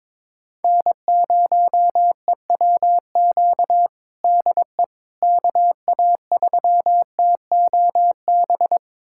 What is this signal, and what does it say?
Morse code, heard as a keyed tone, about 22 wpm, 700 Hz. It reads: N0EWQ DE KA3TOB
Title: USA Callsign DE Callsign 22wpm